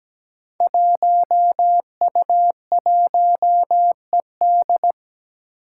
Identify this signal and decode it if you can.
Morse code, heard as a keyed tone, about 17 wpm, 700 Hz. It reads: 1U1ED
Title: Five Challenging Letter-Number Combinations 17wpm